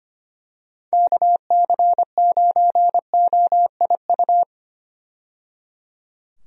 Morse code, heard as a keyed tone, about 25 wpm, 700 Hz. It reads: KC9OIU
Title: USA Call Signs - 6 Characters - 25wpm